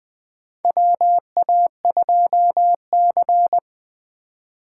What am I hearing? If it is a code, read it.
Morse code, heard as a keyed tone, about 20 wpm, 700 Hz. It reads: WA2C
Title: USA Call Signs - 4 Characters - 3x Character Spacing - 20wpm